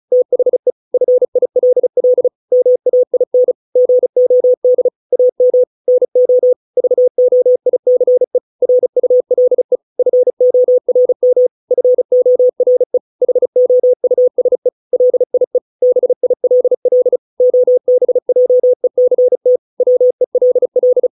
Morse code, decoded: THE FILL MAIN GOD AM NO VOICE RULE FORM FORE HOUSE LIE BILL OBJECT WELL

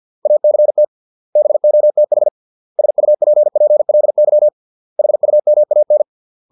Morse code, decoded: ACT BOTH SUPPLY HUMAN

